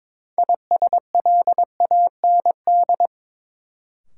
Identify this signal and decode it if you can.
Morse code, heard as a keyed tone, about 22 wpm, 700 Hz. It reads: ISLAND
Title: Sentences from Top 500 Words - New Word Drill - 22wpm